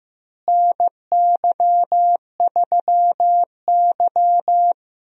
Morse code, heard as a keyed tone, about 15 wpm, 700 Hz. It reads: NY3Y
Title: USA Call Signs - 4 Characters - Repeated 3x - 15wpm